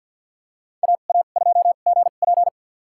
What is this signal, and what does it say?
Morse code, heard as a keyed tone, about 46 wpm, spaced as for 34 wpm, 700 Hz. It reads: AA2GP
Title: USA Call Signs - 5 Characters - 2x Character Spacing - 50wpm